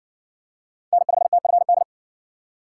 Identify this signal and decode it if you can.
Morse code, heard as a keyed tone, about 56 wpm, 700 Hz. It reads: N5TFD